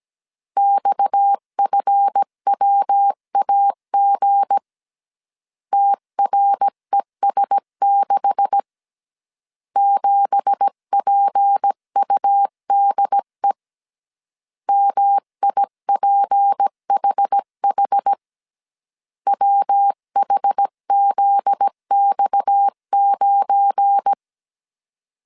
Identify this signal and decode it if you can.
Morse code, heard as a keyed tone, about 17 wpm, 800 Hz. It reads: XFWAG TRES6 7PUDE MIPHH WHZX9
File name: CW / DET / 1 9 1500C S B 16wpm 2856